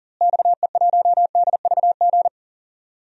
Morse code, heard as a keyed tone, about 40 wpm, 700 Hz. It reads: XE1DVG
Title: DXCC-100 Call Signs - 6 Characters - Repeated 3x - 40wpm